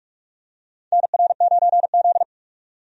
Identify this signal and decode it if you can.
Morse code, heard as a keyed tone, about 46 wpm, 700 Hz. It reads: NR9Z
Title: USA Call Signs - 4 Characters - 45wpm